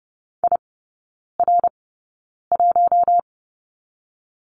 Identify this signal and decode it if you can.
Morse code, heard as a keyed tone, about 30 wpm, spaced as for 9 wpm, 700 Hz. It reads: IR1
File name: Lesson 19e - Letter w - Three Characters 30wpm